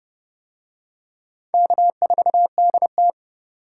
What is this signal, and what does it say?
Morse code, heard as a keyed tone, about 31 wpm, 700 Hz. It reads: K4DT